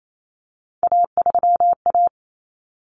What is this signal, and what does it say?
Morse code, heard as a keyed tone, about 28 wpm, 700 Hz. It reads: A3A
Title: Common Cut Numbers - 3-Digit 28wpm